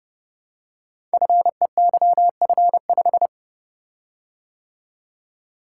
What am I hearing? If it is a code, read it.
Morse code, heard as a keyed tone, about 28 wpm, 700 Hz. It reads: FEYF5
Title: Code Groups - 5 Letter:Numbers - 30wpm